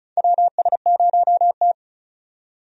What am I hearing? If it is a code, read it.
Morse code, heard as a keyed tone, about 36 wpm, 700 Hz. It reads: WS0T
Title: USA Call Signs - 4 Characters - 35wpm